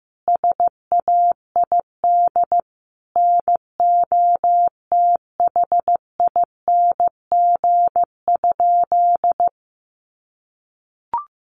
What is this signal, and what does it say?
Morse code, heard as a keyed tone, about 15 wpm, 700 Hz. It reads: SAID NOTHING?
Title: Sentences from Top 500 Words 15wpm